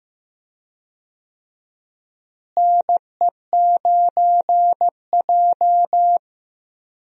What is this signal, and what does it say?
Morse code, heard as a keyed tone, about 15 wpm, 700 Hz. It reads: NE9J
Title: USA Call Signs - 4 Characters - Repeated 3x - 15wpm